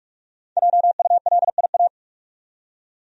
Morse code, heard as a keyed tone, about 47 wpm, 700 Hz. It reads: JULIA
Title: USA Common Names - Max 5 Letters - 45wpm